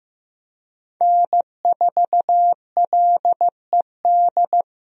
Morse code, heard as a keyed tone, about 15 wpm, 700 Hz. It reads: N4LED
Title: USA Call Signs - 5 Characters - Repeated 2x - 15wpm